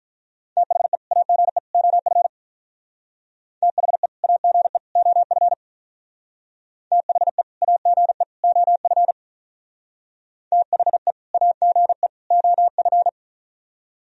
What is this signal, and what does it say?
Morse code, heard as a keyed tone, about 46 wpm, 700 Hz. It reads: THE AGE OF THE AGE OF THE AGE OF THE AGE OF